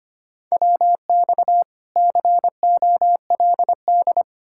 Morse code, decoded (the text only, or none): WX COLD